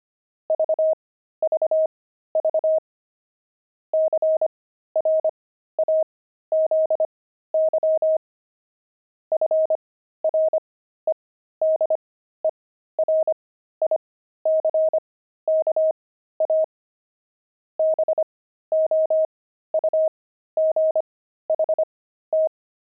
Morse code, heard as a keyed tone, about 25 wpm, spaced as for 13 wpm, 630 Hz. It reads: VVV CRAZY FREDERICKA BOUGHT
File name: pangram-fs-13-cs-25-opal